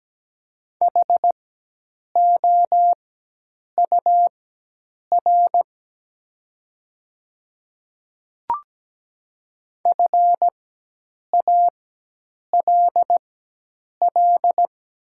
Morse code, heard as a keyed tone, about 17 wpm, spaced as for 8 wpm, 700 Hz. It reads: HOUR FALL
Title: 4x Character Spacing - Top 500 Words - 17wpm